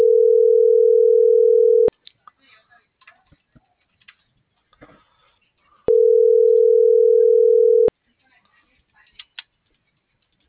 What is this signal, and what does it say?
Ringback tone (North America), 440 + 480 Hz.